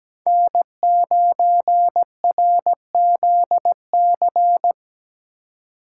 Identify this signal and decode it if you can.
Morse code, heard as a keyed tone, about 17 wpm, 700 Hz. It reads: N9RZC
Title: USA Call Signs - 5 Characters - 3x Character Spacing - 17wpm